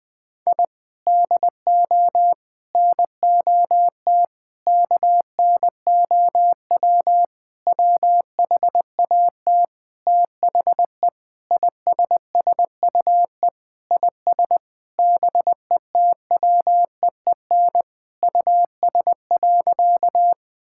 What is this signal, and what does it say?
Morse code, heard as a keyed tone, about 20 wpm, 700 Hz. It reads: I DO NOT KNOW WHAT THE ISSUE IS BETWEEN US.
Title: Sentences from Top 1000 Words - Review Part 6 - 20wpm